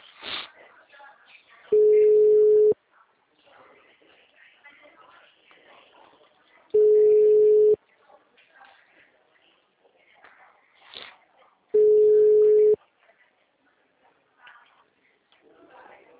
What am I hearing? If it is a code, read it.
Ringback tone (Europe, 425 Hz), with speech.